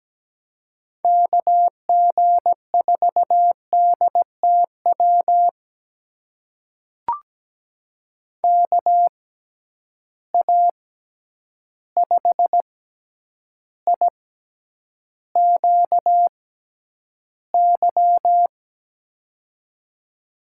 Morse code, heard as a keyed tone, about 17 wpm, spaced as for 9 wpm, 700 Hz. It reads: KG4DTW KA5IQY
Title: USA Call Signs - 6 Characters - 6x Character Spacing - 17wpm